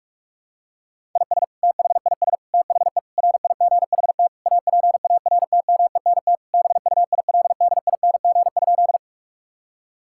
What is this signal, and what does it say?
Morse code, heard as a keyed tone, about 46 wpm, 700 Hz. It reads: IS THIS THE RIGHT APARTMENT BUILDING?
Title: Sentences from Top 1000 Words - Review Part 6 - 45wpm